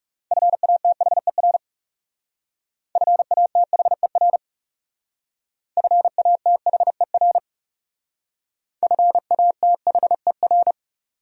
Morse code, heard as a keyed tone, about 38 wpm, 700 Hz. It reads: FATHER FATHER FATHER FATHER
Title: Speed-Racing Top 300 Words 30wpm